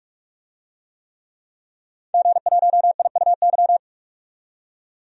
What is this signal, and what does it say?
Morse code, heard as a keyed tone, about 46 wpm, 700 Hz. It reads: G1IUY
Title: DXCC-100 Call Signs - 5 Characters - 45wpm